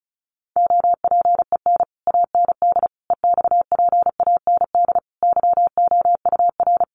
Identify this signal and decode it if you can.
Morse code, heard as a keyed tone, about 34 wpm, 700 Hz. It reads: OPEN AND EXPAND YOUR
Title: Sentences from Top 1400 Words 35wpm